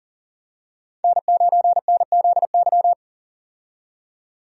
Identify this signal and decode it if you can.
Morse code, heard as a keyed tone, about 40 wpm, 700 Hz. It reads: N9NZY
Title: USA Call Signs - 5 Characters - 40wpm